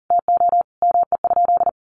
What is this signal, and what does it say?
Morse code, heard as a keyed tone, about 40 wpm, 700 Hz. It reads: TO ME?
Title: Sentences from Top 200 Words 40wpm